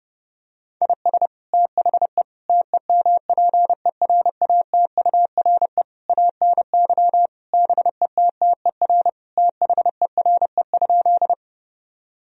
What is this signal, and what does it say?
Morse code, heard as a keyed tone, about 30 wpm, 700 Hz. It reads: IS THE TEMPERATURE ANY BETTER THERE?
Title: Sentences from Top 1000 Words - Review Part 4 - 30wpm